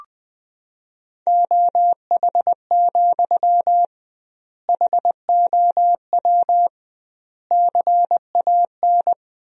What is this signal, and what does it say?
Morse code, heard as a keyed tone, about 20 wpm, 700 Hz. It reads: OH, HOW CAN
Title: 2x Word-Spacing - Sentences from Top 500 Words - 20wpm